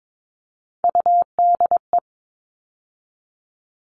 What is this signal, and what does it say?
Morse code, heard as a keyed tone, about 22 wpm, 700 Hz. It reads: UDE
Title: All Cut Numbers - 3-Digit 22wpm